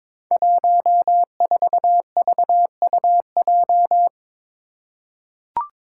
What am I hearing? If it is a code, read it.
Morse code, heard as a keyed tone, about 22 wpm, 700 Hz. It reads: 14VUJ
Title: Five Challenging Letter-Number Combinations 22wpm